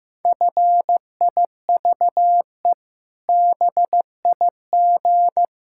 Morse code, heard as a keyed tone, about 16 wpm, 700 Hz. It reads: FIVE BIG
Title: Pangram Challenge v2 - 15wpm